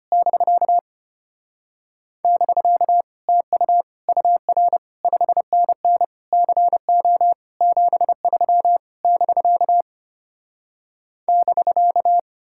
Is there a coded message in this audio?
Yes: Morse code from 0.1 to 12.2 s, heard as a keyed tone, about 33 wpm, 700 Hz. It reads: <BK> <BK> TU UR 5NN CO 73 <BK> <BK>